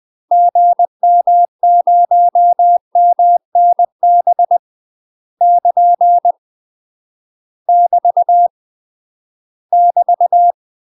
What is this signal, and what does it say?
Morse code, heard as a keyed tone, about 20 wpm, 700 Hz. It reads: GM0MNB <KN> <BT> <BT>
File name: UK MASTER QSOs TRAINING/20WPM